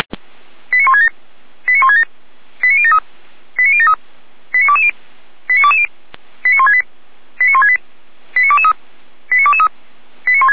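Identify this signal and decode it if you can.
Five-tone selective call (ZVEI), heard as quick runs of five tones, tones of about 70 ms. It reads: 89167 89167 89073 89073 89220 89220 89168 89168 89223 89223
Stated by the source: ZVEI five-tone selective call received with an SDR, logged by Feuerwehr Orlamünde.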